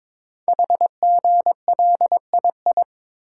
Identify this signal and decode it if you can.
Morse code, heard as a keyed tone, about 22 wpm, 700 Hz. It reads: HGLII
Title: Code Groups - 5 Letters - 22wpm